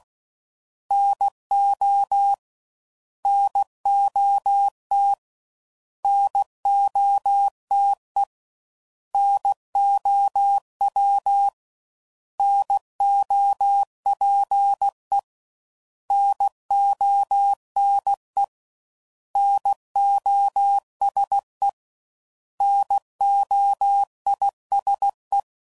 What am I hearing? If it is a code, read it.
Morse code, heard as a keyed tone, about 16 wpm, 800 Hz. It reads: NO NOT NOTE NOW NOPE NONE NOSE NOISE
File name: Prefixes / By First Letter / Ns